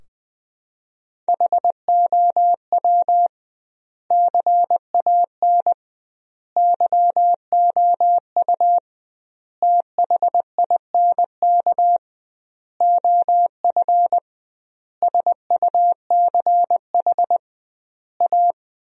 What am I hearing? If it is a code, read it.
Morse code, heard as a keyed tone, about 19 wpm, 700 Hz. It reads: HOW CAN YOU THINK OF SUCH A
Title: 2x Word-Spacing - Sentences from Top 200 Words - 20wpm